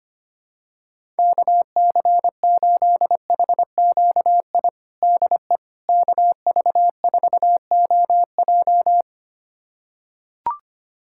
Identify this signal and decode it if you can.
Morse code, heard as a keyed tone, about 25 wpm, 700 Hz. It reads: KC8HQI DE KV4OJ
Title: USA Callsign DE Callsign 25wpm